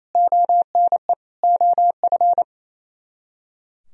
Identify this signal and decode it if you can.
Morse code, heard as a keyed tone, about 33 wpm, 700 Hz. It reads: ONE OF
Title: Speed-Racing - Sets of 4 Words 28wpm